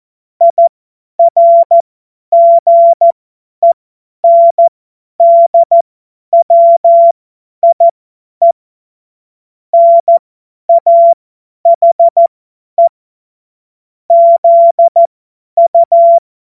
Morse code, heard as a keyed tone, about 14 wpm, spaced as for 10 wpm, 680 Hz. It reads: IRGENDWIE NAHE ZU
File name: Morsekurs / Witze / Witz 22. Juni 2021, 50 F70